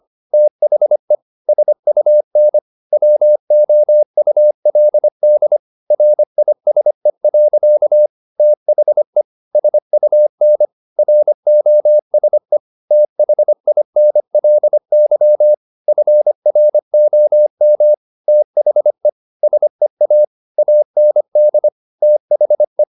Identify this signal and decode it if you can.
Morse code, heard as a keyed tone, about 25 wpm, 600 Hz. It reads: THE SUN WOULD RISE. THE SUN ROSE THINLY FROM THE SEA AND THE
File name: the old man and the sea part 1 25wpm